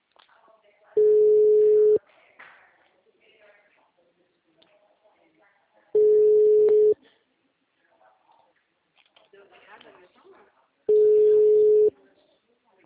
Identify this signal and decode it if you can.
Ringback tone (Europe, 425 Hz), with speech.
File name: out- -2006- - - .849